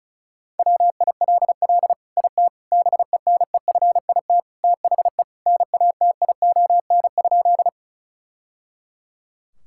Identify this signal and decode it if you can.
Morse code, heard as a keyed tone, about 34 wpm, 700 Hz. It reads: WILL IT BENEFIT THE NATION?
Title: Sentences from Top 1000 Words - Review Part 6 - 35wpm